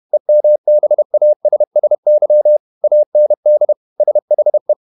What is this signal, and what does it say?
Morse code, heard as a keyed tone, about 31 wpm, 600 Hz. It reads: EMBASSY AND SHE
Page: Never 31wpm | CW med Gnister